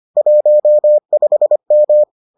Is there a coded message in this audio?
Yes: Morse code from 0.2 to 2.0 s, heard as a keyed tone, about 25 wpm, 600 Hz. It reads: 15M